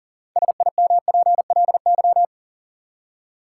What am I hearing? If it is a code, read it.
Morse code, heard as a keyed tone, about 38 wpm, 700 Hz. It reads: SIMPLY
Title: Six Letter Words - 40wpm